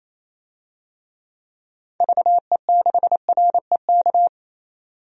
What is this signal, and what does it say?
Morse code, heard as a keyed tone, about 28 wpm, 700 Hz. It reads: VE6REK
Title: DXCC-100 Call Signs - 6 Characters - 28wpm